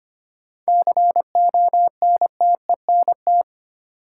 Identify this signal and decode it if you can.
Morse code, heard as a keyed tone, about 25 wpm, 700 Hz. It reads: CONTENT
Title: Sentences from Top 900 Words - New Word Drill - 25wpm